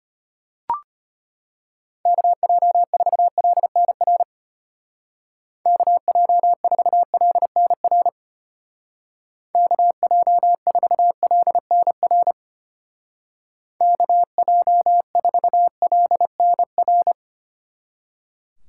Morse code, heard as a keyed tone, about 33 wpm, 700 Hz. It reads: KJ4LNR KJ4LNR KJ4LNR KJ4LNR